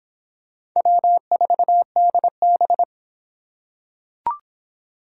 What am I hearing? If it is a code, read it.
Morse code, heard as a keyed tone, about 23 wpm, 700 Hz. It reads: W4DB